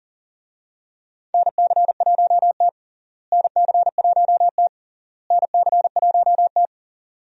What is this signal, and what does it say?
Morse code, heard as a keyed tone, about 40 wpm, 700 Hz. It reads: NC1T NC1T NC1T
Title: USA Call Signs - 4 Characters - Repeated 3x - 40wpm